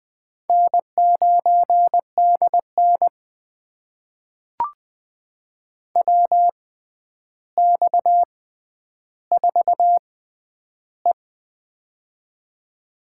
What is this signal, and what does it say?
Morse code, heard as a keyed tone, about 20 wpm, spaced as for 10 wpm, 700 Hz. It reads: N9DN WX4E
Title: USA Call Signs - 4 Characters - 6x Character Spacing - 20wpm